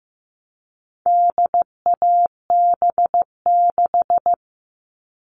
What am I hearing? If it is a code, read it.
Morse code, heard as a keyed tone, about 15 wpm, 700 Hz. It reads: DAB6
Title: All Cut Numbers - 4-Digit 15wpm